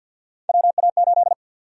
Morse code, heard as a keyed tone, about 47 wpm, 700 Hz. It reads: WA8
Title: USA Call Sign Prefixes - 50wpm